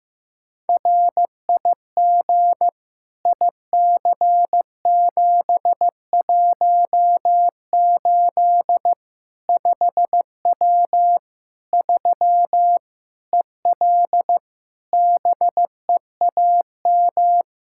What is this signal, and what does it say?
Morse code, heard as a keyed tone, about 15 wpm, 700 Hz. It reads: RIG IC718 5W 3 EL BEAM
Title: K1USN SST QSO Practice 15wpm